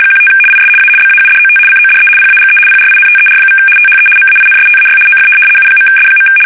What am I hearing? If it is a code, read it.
RTTY, about 75 baud. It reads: 15 IS A BAUDOT CODE PAGE PRINTER= THE MAINSTAY OF U.S.
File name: fsk850-75B / teletype-corp